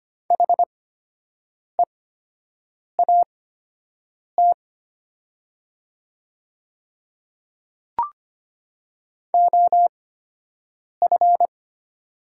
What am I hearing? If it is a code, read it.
Morse code, heard as a keyed tone, about 25 wpm, spaced as for 7 wpm, 700 Hz. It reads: HEAT OF
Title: 8x Character Spacing - Top 500 Words - 25wpm